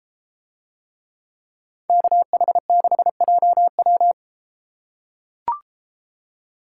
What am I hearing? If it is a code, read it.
Morse code, heard as a keyed tone, about 33 wpm, 700 Hz. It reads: KH6JW